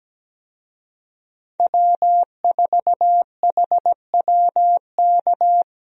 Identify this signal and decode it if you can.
Morse code, heard as a keyed tone, about 17 wpm, 700 Hz. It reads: W4HWK
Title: USA Call Signs - 5 Characters - Repeated 2x - 17wpm